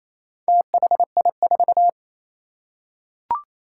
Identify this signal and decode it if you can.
Morse code, heard as a keyed tone, about 28 wpm, 700 Hz. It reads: THI4
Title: Four Challenging Letter-Number Combinations 28wpm